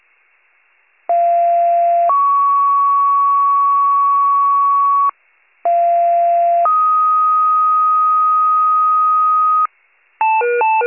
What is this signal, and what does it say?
Two-tone page, heard as a one-second tone, then a three-second tone, about 690 then 1120 Hz, about 690 then 1250 Hz. It is heard from 1.1 to 9.7 s.